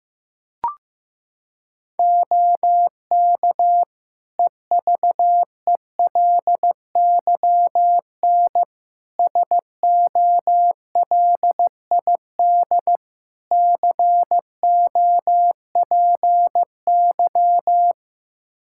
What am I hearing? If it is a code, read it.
Morse code, heard as a keyed tone, about 15 wpm, 700 Hz. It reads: OK EVELYN SOLID COPY
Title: Practice QSO - Exchange 3 - More Info - 15wpm